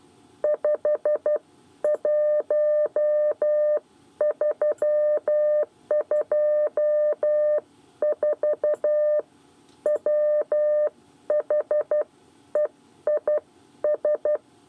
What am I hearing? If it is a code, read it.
Morse code, heard as a keyed tone, about 12 wpm, spaced as for 10 wpm, 600 Hz. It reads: 51324 WHEIS
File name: 8 bis lezione cw 1